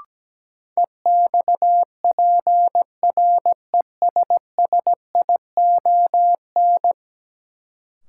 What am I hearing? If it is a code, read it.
Morse code, heard as a keyed tone, about 17 wpm, 700 Hz. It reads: EXPRESSION
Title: Sentences from Top 1200 Words - New Word Drill Alt - 17wpm